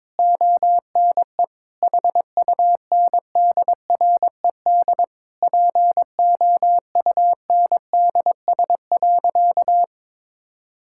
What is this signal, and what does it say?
Morse code, heard as a keyed tone, about 22 wpm, 700 Hz. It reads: ONE HUNDRED POUNDS.
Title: Sentences from Top 1000 Words - Review Part 6 - 22wpm